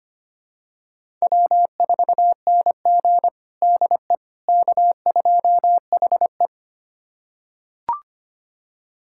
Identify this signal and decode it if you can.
Morse code, heard as a keyed tone, about 25 wpm, 700 Hz. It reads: W4NG DE K2HE
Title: USA Callsign DE Callsign - 4 Characters - 25wpm